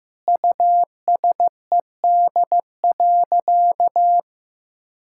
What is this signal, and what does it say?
Morse code, heard as a keyed tone, about 15 wpm, 700 Hz. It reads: USED.
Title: Sentences from Top 1600 Words 15wpm